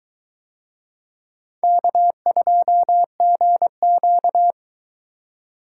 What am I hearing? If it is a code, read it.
Morse code, heard as a keyed tone, about 22 wpm, 700 Hz. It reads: K2GQ